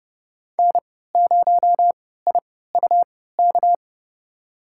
Morse code, heard as a keyed tone, about 30 wpm, spaced as for 17 wpm, 700 Hz. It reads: N0IUK
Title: USA Call Signs - 5 Characters - 3x Character Spacing - 30wpm